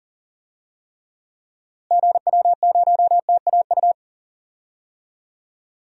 Morse code, heard as a keyed tone, about 40 wpm, 700 Hz. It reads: GW0TAU
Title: DXCC-100 Call Signs - 6 Characters - Repeated 3x - 40wpm